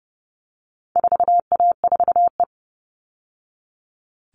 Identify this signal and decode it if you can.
Morse code, heard as a keyed tone, about 30 wpm, 700 Hz. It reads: 4A4E
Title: All Cut Numbers - 4-Digit 30wpm